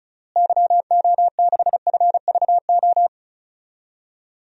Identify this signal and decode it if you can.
Morse code, heard as a keyed tone, about 36 wpm, 700 Hz. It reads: YO6FVO
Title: DXCC-100 Call Signs - 6 Characters - Repeated 3x - 35wpm